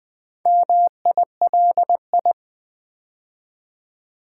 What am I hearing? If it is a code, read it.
Morse code, heard as a keyed tone, about 20 wpm, 700 Hz. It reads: MILI
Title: Build A Word - 2 Letters At A Time - 8-Letter Words - 20wpm